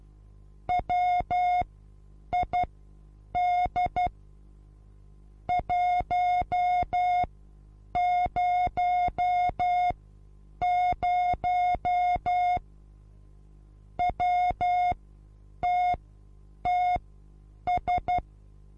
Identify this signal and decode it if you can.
Morse code, heard as a keyed tone, about 12 wpm, spaced as for 8 wpm, 720 Hz. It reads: WID 100 WTTS
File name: Onlinelehrgang / morsen / DJ4UF Morsen23